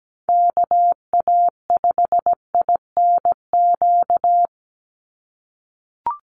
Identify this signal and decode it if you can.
Morse code, heard as a keyed tone, about 17 wpm, 700 Hz. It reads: KA5INQ
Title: USA Call Signs 17wpm